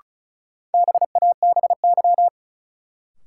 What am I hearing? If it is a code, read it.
Morse code, heard as a keyed tone, about 34 wpm, 700 Hz. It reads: BABY
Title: Sentences from Top 1300 Words - New Word Drill Alt - 35wpm